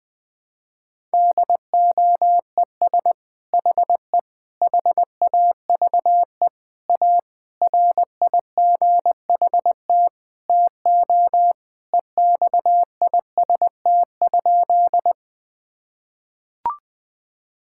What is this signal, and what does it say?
Morse code, heard as a keyed tone, about 20 wpm, 700 Hz. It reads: DOES HE HAVE A RIGHT TO EXIST?
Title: Sentences from Top 1000 Words - Review Part 5 - 20wpm